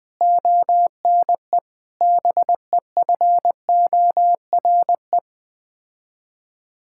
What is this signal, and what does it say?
Morse code, heard as a keyed tone, about 20 wpm, 700 Hz. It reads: ONE BEFORE
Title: Sets of 8 Words - 20wpm